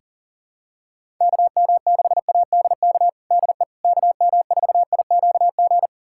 Morse code, heard as a keyed tone, about 40 wpm, 700 Hz. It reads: KM6ADK DE KM4IQG
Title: USA Callsign DE Callsign - 6 Characters - 40wpm